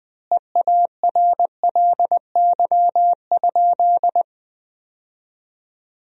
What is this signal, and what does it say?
Morse code, heard as a keyed tone, about 20 wpm, 700 Hz. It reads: EARLY?
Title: 2x Word-Spacing - Sentences from Top 400 Words - 20wpm